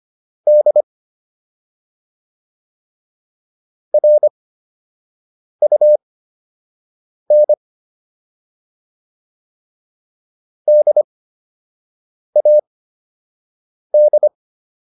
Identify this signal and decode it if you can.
Morse code, heard as a keyed tone, about 25 wpm, spaced as for 6 wpm, 600 Hz. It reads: D RUN DAD